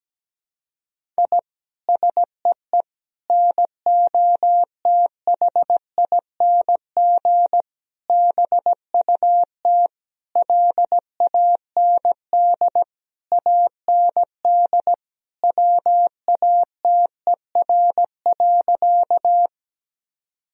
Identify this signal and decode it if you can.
Morse code, heard as a keyed tone, about 17 wpm, 700 Hz. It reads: I SEE NOTHING BUT LAND AND WATER.
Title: Sentences from Top 1000 Words - Review Part 1 - 17wpm